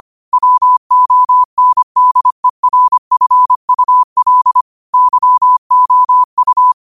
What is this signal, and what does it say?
Morse code, heard as a keyed tone, about 25 wpm, 1000 Hz. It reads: WONDERFUL YOU
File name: qotdfiles / 25wpm